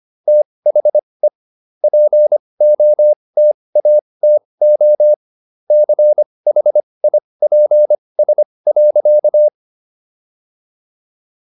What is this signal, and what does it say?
Morse code, heard as a keyed tone, about 25 wpm, spaced as for 20 wpm, 600 Hz. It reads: THE POTATO CHIPS.